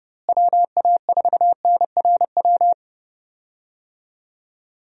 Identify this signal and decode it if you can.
Morse code, heard as a keyed tone, about 30 wpm, 700 Hz. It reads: WA4NRW
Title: USA Call Signs - 6 Characters - Repeated 3x - 30wpm